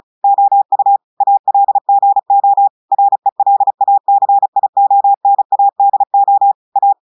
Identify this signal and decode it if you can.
Morse code, heard as a keyed tone, about 36 wpm, 800 Hz. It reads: OU ALGO RELACIONADO A
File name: audios / portugais / portugais texte en clair 2 vitesse 35 mots minutes